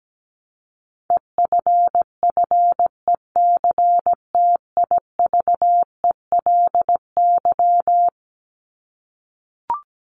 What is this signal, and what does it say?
Morse code, heard as a keyed tone, about 17 wpm, 700 Hz. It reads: EFFECTIVELY
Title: LY Suffix - Top 100 Words with Base Words 17wpm